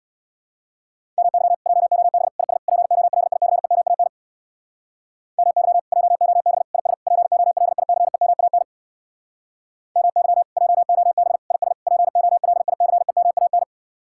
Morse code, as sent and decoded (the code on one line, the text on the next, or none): -- -.-- / .--- --- -... / .. ... / .--. --- .-.. .. -.-. -- .- -. / -- -.-- / .--- --- -... / .. ... / .--. --- .-.. .. -.-. -- .- -. / -- -.-- / .--- --- -... / .. ... / .--. --- .-.. .. -.-. -- .- -.
MY JOB IS POLICEMAN MY JOB IS POLICEMAN MY JOB IS POLICEMAN